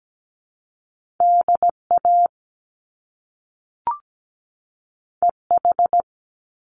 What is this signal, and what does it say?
Morse code, heard as a keyed tone, about 17 wpm, 700 Hz. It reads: DA EH
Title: Top Two Letter Combinations in Words 17wpm